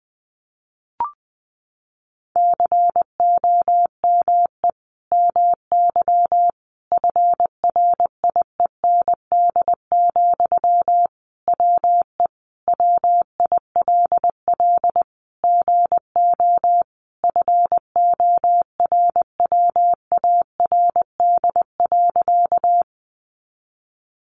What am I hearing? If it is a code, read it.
Morse code, heard as a keyed tone, about 20 wpm, 700 Hz. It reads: COME MY FRIEND, WE WILL GO FORWARD.
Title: Sentences from Top 1300 Words 20wpm